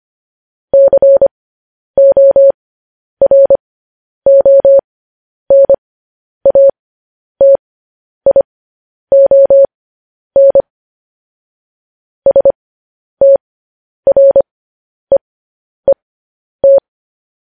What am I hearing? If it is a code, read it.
Morse code, heard as a keyed tone, about 25 wpm, spaced as for 10 wpm, 550 Hz. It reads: CORONATION STREET
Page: news headlines in morse code at 10 wpm